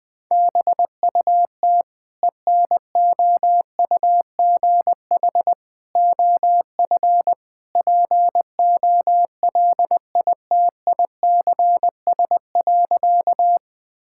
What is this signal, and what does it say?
Morse code, heard as a keyed tone, about 20 wpm, 700 Hz. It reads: BUT ENOUGH OF POLITICS.
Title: Sentences from Top 1000 Words - Review Part 6 - 20wpm